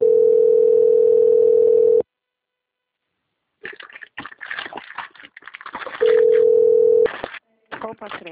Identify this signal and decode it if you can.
Ringback tone (North America), 440 + 480 Hz, with speech.
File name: out- - - - - .20182